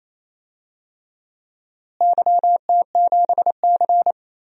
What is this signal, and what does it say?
Morse code, heard as a keyed tone, about 28 wpm, 700 Hz. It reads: YT7C
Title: DXCC-100 Call Signs - 4 Characters - Repeated 3x - 28wpm